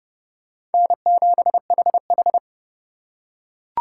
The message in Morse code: N7HH

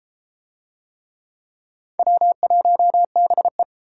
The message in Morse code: W1BE

